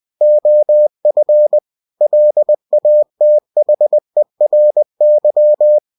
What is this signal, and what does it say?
Morse code, heard as a keyed tone, about 20 wpm, 600 Hz. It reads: OF LATHERY
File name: cw-books / winnie-the-pooh / winnie-20wpm-page-0072